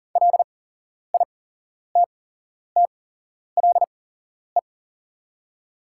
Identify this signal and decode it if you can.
Morse code, heard as a keyed tone, about 40 wpm, spaced as for 11 wpm, 700 Hz. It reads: LITTLE
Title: 8x Character Spacing - Top 300 Words - 40wpm